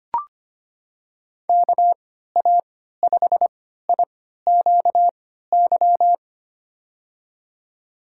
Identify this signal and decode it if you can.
Morse code, heard as a keyed tone, about 25 wpm, spaced as for 14 wpm, 690 Hz. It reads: KA5IQY
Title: USA Call Signs - 6 Characters - 3x Character Spacing - 25wpm